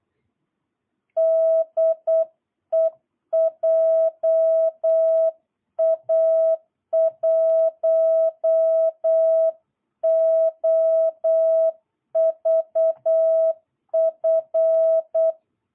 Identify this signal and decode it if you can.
Morse code, heard as a keyed tone, about 8 wpm, 660 Hz. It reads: DE JA1OVF
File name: jh8cbh / 2022cw